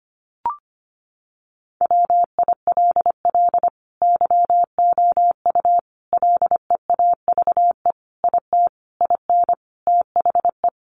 Morse code, heard as a keyed tone, about 25 wpm, 700 Hz. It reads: WILL YOU LEAVE IT IN THE
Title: Sentences from Top 1300 Words 25wpm